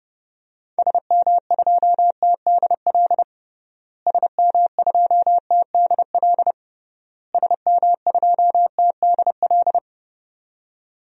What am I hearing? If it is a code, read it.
Morse code, heard as a keyed tone, about 30 wpm, 700 Hz. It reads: SM2TDL SM2TDL SM2TDL
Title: DXCC-100 Call Signs - 6 Characters - Repeated 3x - 30wpm